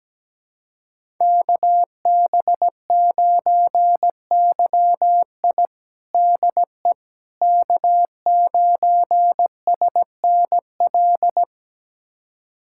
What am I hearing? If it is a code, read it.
Morse code, heard as a keyed tone, about 17 wpm, 700 Hz. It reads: KB9YI DE K9SNL
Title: USA Callsign DE Callsign - 5 Characters - 17wpm